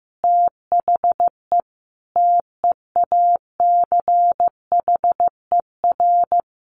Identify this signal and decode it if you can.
Morse code, heard as a keyed tone, about 15 wpm, 700 Hz. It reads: THE TEACHER
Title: Additional Easy Phrases 15wpm